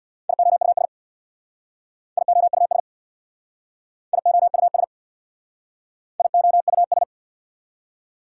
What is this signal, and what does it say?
Morse code, heard as a keyed tone, about 65 wpm, 690 Hz. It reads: IOUS IOUS IOUS IOUS